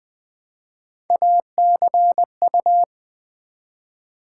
Morse code, heard as a keyed tone, about 20 wpm, 700 Hz. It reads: ACU
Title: Build A Word - 1 Letter At A Time - 3 to 5-Letter Words - 20wpm